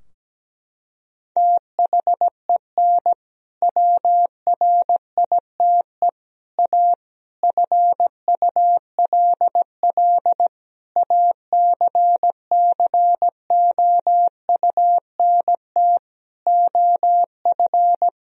Morse code, heard as a keyed tone, about 17 wpm, 700 Hz. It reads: THEN WRITE A FULL ACCOUNT OF
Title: Sentences from Top 1000 Words - Review Part 3 - 17wpm